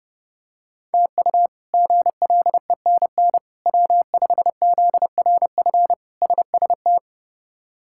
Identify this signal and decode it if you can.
Morse code, heard as a keyed tone, about 31 wpm, 700 Hz. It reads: TU GLENN W5ZRF SST
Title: K1USN SST QSO Practice - 30wpm